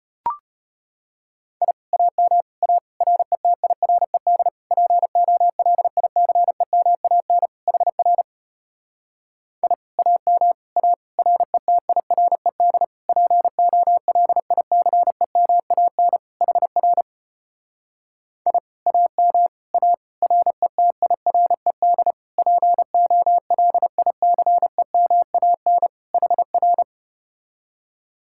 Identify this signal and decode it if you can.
Morse code, heard as a keyed tone, about 33 wpm, 700 Hz. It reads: I AM A RETIRED POLICEMAN HR I AM A RETIRED POLICEMAN HR I AM A RETIRED POLICEMAN HR